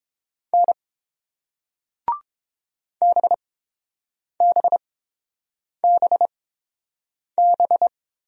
Morse code, heard as a keyed tone, about 30 wpm, spaced as for 7 wpm, 700 Hz. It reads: N BBBB